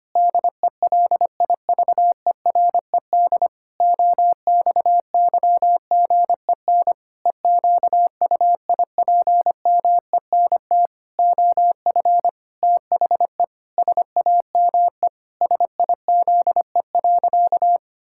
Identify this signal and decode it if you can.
Morse code, heard as a keyed tone, about 25 wpm, 700 Hz. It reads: DELIVERED OXYGEN EQUIPMENT OF THE SAME SIZE.
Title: Pangram Challenge v2 - 25wpm